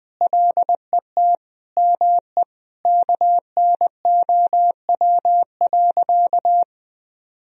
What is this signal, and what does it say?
Morse code, heard as a keyed tone, about 21 wpm, 700 Hz. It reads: LET ME KNOW.
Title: Sentences from Top 1000 Words - Review Part 2 - 20wpm